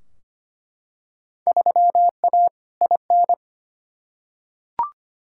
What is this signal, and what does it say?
Morse code, heard as a keyed tone, about 25 wpm, 700 Hz. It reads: 3A IN
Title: Field Day Exchanges - 25wpm